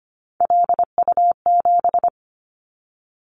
Morse code, heard as a keyed tone, about 25 wpm, 700 Hz. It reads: LU7
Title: Callsign Prefixes for Top 100 DXCC - 25wpm